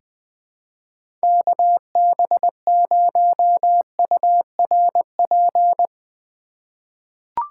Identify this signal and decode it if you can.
Morse code, heard as a keyed tone, about 20 wpm, 700 Hz. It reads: KB0URP
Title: USA Call Signs - 2x Character Spacing - 20wpm